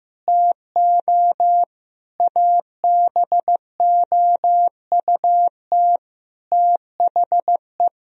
Morse code, heard as a keyed tone, about 15 wpm, 700 Hz. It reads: TO ABOUT THE
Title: Sentences from Top 1000 Words - Review Part 4 - 15wpm